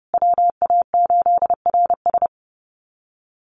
Morse code, heard as a keyed tone, about 30 wpm, 700 Hz. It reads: WA8RS
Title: USA Call Signs Repeated 3x 30wpm